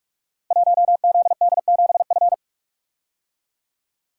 Morse code, heard as a keyed tone, about 44 wpm, 700 Hz. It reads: 1ZD7F
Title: Code Groups - 5 Letter:Numbers - 45wpm